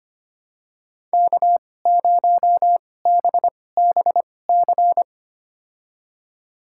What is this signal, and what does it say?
Morse code, heard as a keyed tone, about 25 wpm, spaced as for 18 wpm, 700 Hz. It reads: K0BBC
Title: USA Call Signs - 5 Characters - 2x Character Spacing - 25wpm